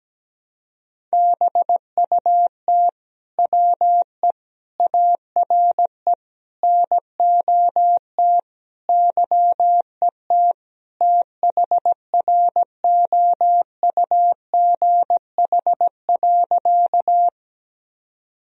Morse code, heard as a keyed tone, about 17 wpm, 700 Hz. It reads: BUT WE ARE NOT YET THROUGH.
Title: Sentences from Top 1000 Words - Review Part 1 - 17wpm